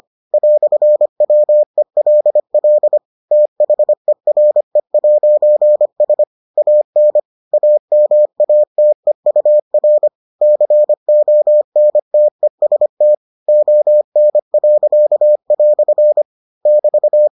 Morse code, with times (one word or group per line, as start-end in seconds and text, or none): 0.3-3.0 "WELL
3.3-6.2 THERE'S
6.6-7.2 AN
7.5-10.1 AMATEUR
10.4-13.1 CONTEST
13.5-16.2 ON."
16.7-17.3 <BT>